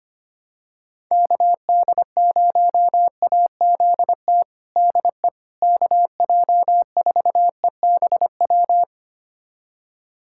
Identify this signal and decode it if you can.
Morse code, heard as a keyed tone, about 25 wpm, 700 Hz. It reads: KD0AZT DE KJ4EBW
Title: USA Callsign DE Callsign 25wpm